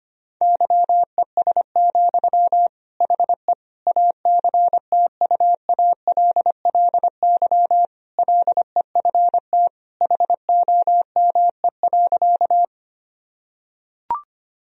Morse code, heard as a keyed tone, about 25 wpm, 700 Hz. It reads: YES, HE ACTUALLY LEFT HOME.
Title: Sentences from Top 1000 Words - Review Part 4 - 25wpm